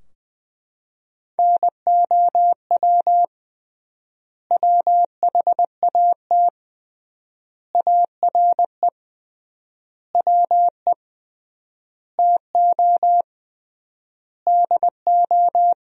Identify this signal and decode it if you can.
Morse code, heard as a keyed tone, about 20 wpm, 700 Hz. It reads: NOW WHAT ARE WE TO DO
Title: 3x Word-Spacing - Sentences from Top 100 Words - 20wpm